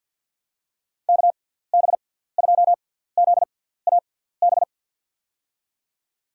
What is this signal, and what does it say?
Morse code, heard as a keyed tone, about 53 wpm, spaced as for 17 wpm, 700 Hz. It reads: KB2ZAB
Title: USA Call Signs - 6 Characters - 6x Character Spacing - 50wpm